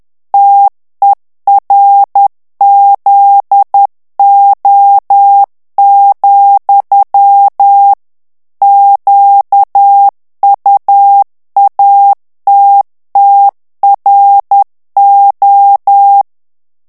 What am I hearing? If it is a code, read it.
Morse code, heard as a keyed tone, about 10 wpm, 800 Hz. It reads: TERZO, QUATTRO
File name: ITALIEN texte clair 2 vitesse 10 mots minute